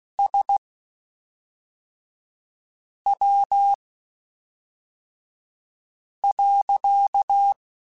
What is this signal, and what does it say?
Morse code, heard as a keyed tone, about 16 wpm, spaced as for 3 wpm, 800 Hz. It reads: SW.
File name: Spaced Characters / Spaced Characters-mp3s / Weather5 5 sp @5